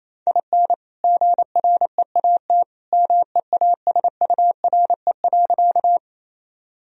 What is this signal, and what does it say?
Morse code, heard as a keyed tone, about 27 wpm, 700 Hz. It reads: IN GREAT MEASURE.
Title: Sentences from Top 1000 Words - Review Part 2 - 28wpm